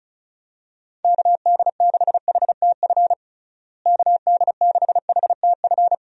Morse code, heard as a keyed tone, about 36 wpm, 700 Hz. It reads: KD6HTF KD6HTF
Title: USA Call Signs - 6 Characters - Repeated 2x - 35wpm